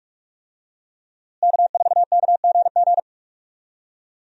Morse code, heard as a keyed tone, about 46 wpm, 700 Hz. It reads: K4KGG
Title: USA Call Signs - 5 Characters - Repeated 2x - 45wpm